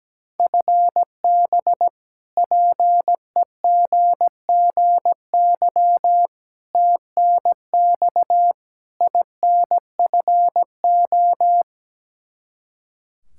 Morse code, heard as a keyed tone, about 17 wpm, 700 Hz. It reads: FB PEGGY TNX INFO
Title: Practice QSO - Exchange 3 - More Info - 17wpm